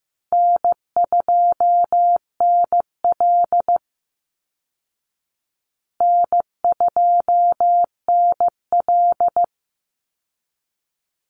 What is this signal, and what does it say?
Morse code, heard as a keyed tone, about 15 wpm, 700 Hz. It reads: N2NL N2NL
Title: Call Signs Repeated 2x - 15wpm